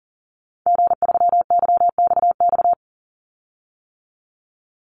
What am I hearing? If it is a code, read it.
Morse code, heard as a keyed tone, about 40 wpm, 700 Hz. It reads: G3YXX
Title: Call Signs - 40wpm